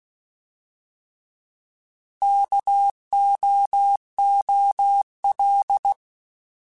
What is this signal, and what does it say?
Morse code, heard as a keyed tone, about 16 wpm, 800 Hz. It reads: KOOL
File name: Alphabetical / Alphabetical mp3s / Ks @15